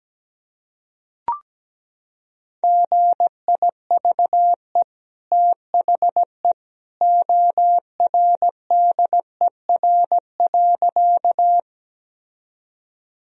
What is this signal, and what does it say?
Morse code, heard as a keyed tone, about 17 wpm, 700 Hz. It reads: GIVE THE ORDER.
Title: Sentences from Top 1000 Words - Review Part 2 - 17wpm